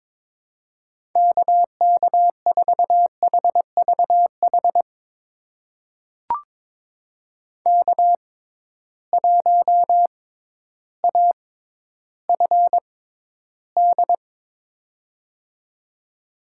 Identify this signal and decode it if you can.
Morse code, heard as a keyed tone, about 22 wpm, spaced as for 12 wpm, 700 Hz. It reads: KK4HVH K1AFD
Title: USA Call Signs - 6x Character Spacing - 22wpm